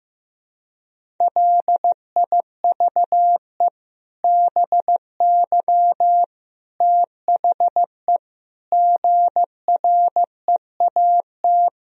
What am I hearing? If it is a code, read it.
Morse code, heard as a keyed tone, about 15 wpm, 700 Hz. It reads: LIVE BY THE GREAT
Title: Sentences from Top 1000 Words - Review Part 3 - 15wpm